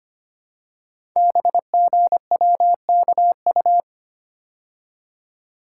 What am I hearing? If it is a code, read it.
Morse code, heard as a keyed tone, about 25 wpm, 700 Hz. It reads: BGWKU
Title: Code Groups - 5 Letters - 25wpm